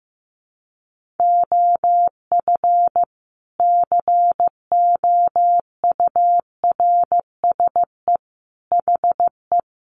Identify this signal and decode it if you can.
Morse code, heard as a keyed tone, about 15 wpm, 700 Hz. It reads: OF COURSE HE
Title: Sentences from Top 1200 Words 15wpm